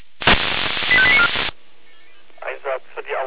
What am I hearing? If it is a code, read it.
Five-tone selective call (ZVEI), heard as quick runs of five tones, tones of about 70 ms, with speech. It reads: 85894